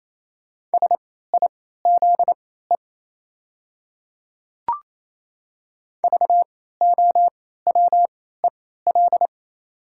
Morse code, heard as a keyed tone, about 27 wpm, spaced as for 16 wpm, 690 Hz. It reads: SIZE VOWEL
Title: 3x Character Spacing - Top 500 Words - 28wpm